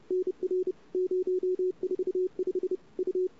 Morse code, heard as a keyed tone, about 31 wpm, 380 Hz. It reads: NR 045 U
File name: CW / 2008-ss-cw-k5tr-op-n5rz -22:00